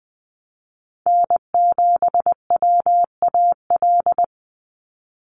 Morse code, heard as a keyed tone, about 20 wpm, 700 Hz. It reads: N7WAL
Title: USA Call Signs 20wpm